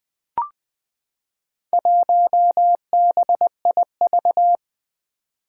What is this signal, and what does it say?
Morse code, heard as a keyed tone, about 20 wpm, 700 Hz. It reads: 1BIV
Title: Four Challenging Letter-Number Combinations 20wpm